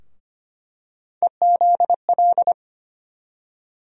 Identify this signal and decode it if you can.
Morse code, heard as a keyed tone, about 25 wpm, 700 Hz. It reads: EZL
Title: USA Call Sign Postfixes - 3 Letters 25wpm